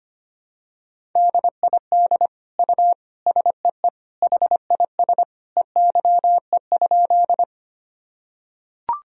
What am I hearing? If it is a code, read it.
Morse code, heard as a keyed tone, about 25 wpm, 700 Hz. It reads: DID U SEE HIS EYE?
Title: Sentences from Top 300 Words and Abbreviations 25wpm